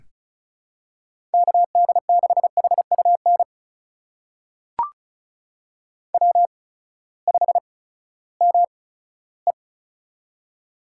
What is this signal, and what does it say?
Morse code, heard as a keyed tone, about 36 wpm, spaced as for 18 wpm, 700 Hz. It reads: KD6HUN W5ME
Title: USA Call Signs - 8x Character Spacing - 35wpm